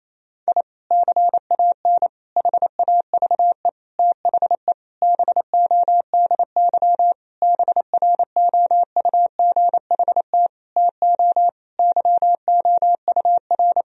Morse code, heard as a keyed tone, about 28 wpm, 700 Hz. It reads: I CAN HAVE THE BODY BROUGHT TO YOUR
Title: Sentences from Top 1000 Words - Review Part 2 - 28wpm